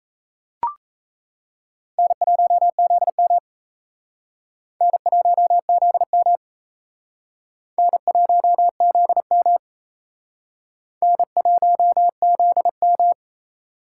Morse code, heard as a keyed tone, about 34 wpm, 700 Hz. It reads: N1ZM N1ZM N1ZM N1ZM
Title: Speed Racing - USA 1x2 Call Signs 28wpm